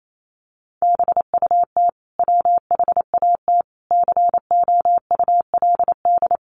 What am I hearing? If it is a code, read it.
Morse code, heard as a keyed tone, about 28 wpm, 700 Hz. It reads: BUT WHAT COULD
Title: Sentences from Top 1300 Words 28wpm